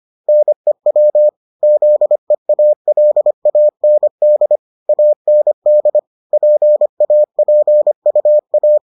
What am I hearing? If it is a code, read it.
Morse code, heard as a keyed tone, about 25 wpm, 600 Hz. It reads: NEW ZEALAND AND PAPUA